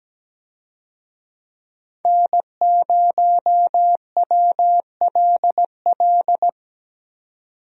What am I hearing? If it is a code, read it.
Morse code, heard as a keyed tone, about 17 wpm, 700 Hz. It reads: N0WLL
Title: USA Call Signs - 5 Characters - 17wpm